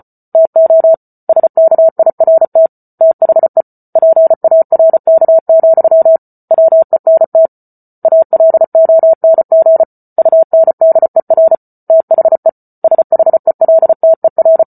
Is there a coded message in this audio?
Yes: Morse code from 0.3 to 14.6 s, heard as a keyed tone, about 34 wpm, 650 Hz. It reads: TO SKIRT THE PARK, WENT ALONG UNDER THE SHELTER